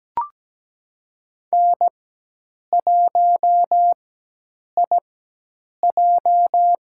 Morse code, heard as a keyed tone, about 17 wpm, spaced as for 8 wpm, 700 Hz. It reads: N1IJ
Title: USA Call Signs - 4x Character Spacing - 17wpm